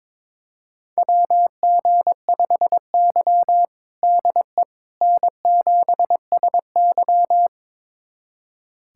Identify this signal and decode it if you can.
Morse code, heard as a keyed tone, about 22 wpm, 700 Hz. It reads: WG5Y DE N7SY
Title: USA Callsign DE Callsign - 4 Characters - 22wpm